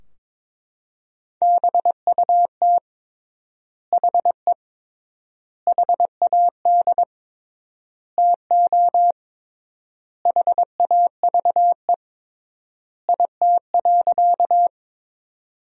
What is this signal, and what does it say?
Morse code, heard as a keyed tone, about 22 wpm, 700 Hz. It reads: BUT HE HAD TO HAVE IT.
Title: 3x Word-Spacing - Sentences from Top 100 Words - 22wpm